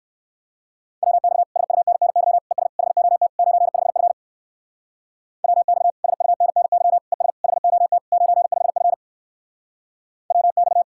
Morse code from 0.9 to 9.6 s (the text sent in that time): WX SUNNY ES HOT 95F WX SUNNY ES HOT 95F